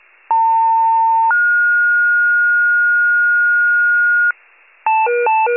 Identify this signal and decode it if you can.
Two-tone page, heard as a one-second tone, then a three-second tone, about 900 then 1430 Hz, with noise. It is heard from 0.3 to 4.3 s.